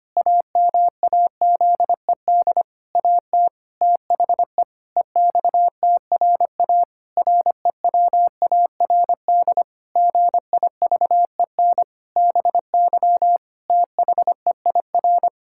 Morse code, heard as a keyed tone, about 25 wpm, 700 Hz. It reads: AMAZED AT THE EXTRA REWARD GIVEN BY THEIR
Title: Pangram Challenge v2 - 25wpm